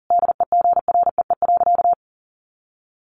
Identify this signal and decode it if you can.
Morse code, heard as a keyed tone, about 40 wpm, 700 Hz. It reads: DEGREE.
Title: Sentences from Top 1500 Words 40wpm